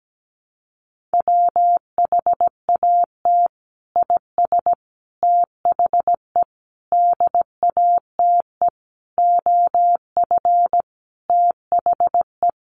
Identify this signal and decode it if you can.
Morse code, heard as a keyed tone, about 17 wpm, 700 Hz. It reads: WHAT IS THE DATE OF THE
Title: Sentences from Top 1100 Words 17wpm